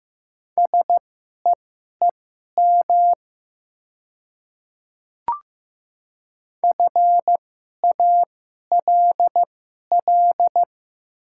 Morse code, heard as a keyed tone, about 15 wpm, spaced as for 11 wpm, 700 Hz. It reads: SEEM FALL
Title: 2x Character Spacing - Top 500 Words - 15wpm